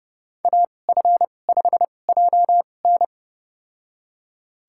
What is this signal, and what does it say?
Morse code, heard as a keyed tone, about 32 wpm, spaced as for 22 wpm, 700 Hz. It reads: AF5JN
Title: USA Call Signs - 5 Characters - 2x Character Spacing - 30wpm